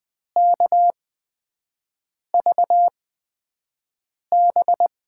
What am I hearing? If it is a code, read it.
Morse code, heard as a keyed tone, about 20 wpm, spaced as for 5 wpm, 700 Hz. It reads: KVB
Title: USA Call Signs - 6 Characters - 8x Character Spacing - 20wpm